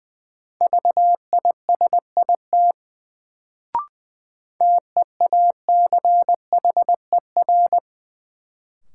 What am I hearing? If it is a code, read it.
Morse code, heard as a keyed tone, about 20 wpm, 700 Hz. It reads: VISIT TEACHER
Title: Sentences from Top 900 Words - New Word Drill - 20wpm